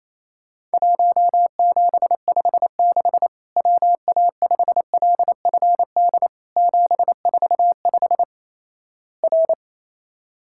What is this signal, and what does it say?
Morse code, heard as a keyed tone, about 28 wpm, 700 Hz. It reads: 1756 WA5LFD 745 R
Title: Worked All Europe DX Contest - QTC Exchange - 28wpm